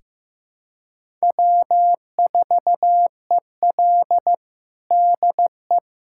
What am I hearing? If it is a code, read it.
Morse code, heard as a keyed tone, about 15 wpm, 700 Hz. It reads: W4EL DE
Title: USA Callsign DE Callsign - 4 Characters - 15wpm